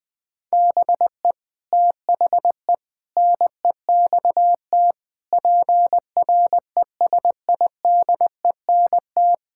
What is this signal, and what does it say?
Morse code, heard as a keyed tone, about 20 wpm, 700 Hz. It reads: BE THE NEXT PRESIDENT
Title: Sets of 8 Words - 20wpm